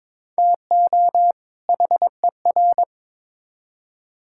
Morse code, heard as a keyed tone, about 21 wpm, 700 Hz. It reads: TO HER
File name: Build A Phrase - 2-Word Phrases - 22wpm